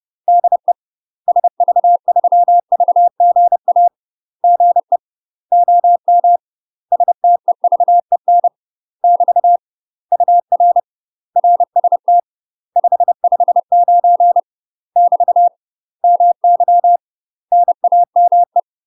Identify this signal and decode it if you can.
Morse code, heard as a keyed tone, about 30 wpm, 700 Hz. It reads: DE SV3VGA GE OM STEVEN <BT> UR RST 559 <BT> MY NAME
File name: UK MASTER QSOs TRAINING / 30WPM / UK MASTER QSOs 0041